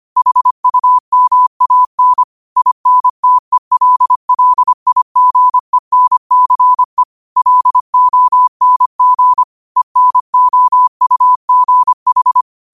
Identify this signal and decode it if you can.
Morse code, heard as a keyed tone, about 25 wpm, 1000 Hz. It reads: HUMAN INTELLIGENCE LONG ENOUGH
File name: qotdfiles / 25wpm